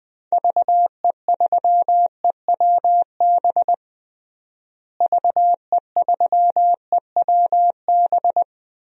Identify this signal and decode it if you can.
Morse code, heard as a keyed tone, about 20 wpm, 700 Hz. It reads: VE3EWB VE3EWB
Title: DXCC-100 Call Signs - 6 Characters - Repeated 2x - 20wpm